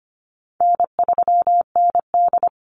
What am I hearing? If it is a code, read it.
Morse code, heard as a keyed tone, about 25 wpm, 700 Hz. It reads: N3ND
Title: Easy Call Signs Repeated 3x - 25wpm